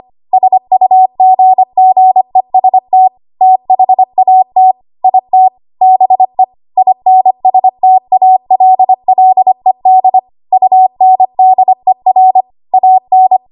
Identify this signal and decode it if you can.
Morse code, heard as a keyed tone, about 25 wpm, 750 Hz. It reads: SUGGEST THAT IT BE INSTALLED UNDER AN
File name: Archive / 25 WPM / 25WPM